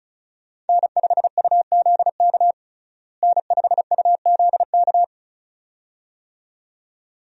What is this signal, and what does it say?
Morse code, heard as a keyed tone, about 36 wpm, 700 Hz. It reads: N5UZK N5UZK
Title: USA Call Signs - 5 Characters - Repeated 2x - 35wpm